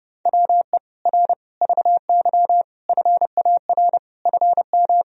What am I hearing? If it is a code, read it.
Morse code, heard as a keyed tone, about 30 wpm, 700 Hz. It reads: WE R VY FAR FM
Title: Sentences from Top 300 Words and Abbreviations 30wpm